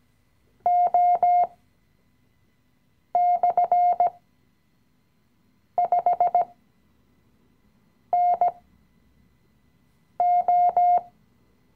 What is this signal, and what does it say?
Morse code, heard as a keyed tone, about 17 wpm, spaced as for 5 wpm, 710 Hz. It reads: O/5NO
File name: CW-Online-Kurs / Anfänger / Lektion 018